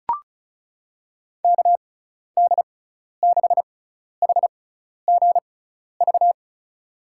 Morse code, heard as a keyed tone, about 34 wpm, spaced as for 12 wpm, 700 Hz. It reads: KD6HGV
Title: USA Call Signs - 6 Characters - 6x Character Spacing - 35wpm